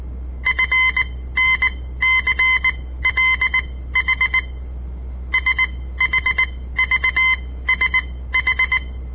Morse code, heard as a keyed tone, about 19 wpm, spaced as for 14 wpm, 2050 Hz. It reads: FNCLH SHVSH
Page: On the air Morse code practice from the Southern Montana Amatuer Radio Association; all code, no read back